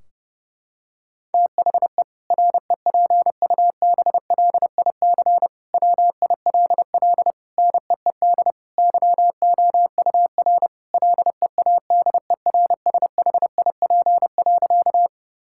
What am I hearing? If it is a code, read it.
Morse code, heard as a keyed tone, about 30 wpm, 700 Hz. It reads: THE REPUBLIC WILL NEED YOUR LEADERSHIP.
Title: Sentences from Top 1600 Words 30wpm